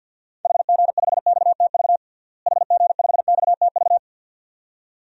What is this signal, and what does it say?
Morse code, heard as a keyed tone, about 47 wpm, 700 Hz. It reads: HG5XTV HG5XTV
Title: DXCC-100 Call Signs - 6 Characters - Repeated 2x - 50wpm